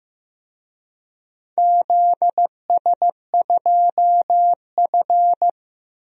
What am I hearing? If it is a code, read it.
Morse code, heard as a keyed tone, about 15 wpm, 700 Hz. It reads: ZS2F
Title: DXCC-100 Call Signs - 4 Characters - 15wpm